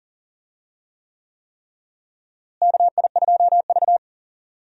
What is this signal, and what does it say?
Morse code, heard as a keyed tone, about 39 wpm, 700 Hz. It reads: KI2V
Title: USA Call Signs - 4 Characters - Repeated 3x - 40wpm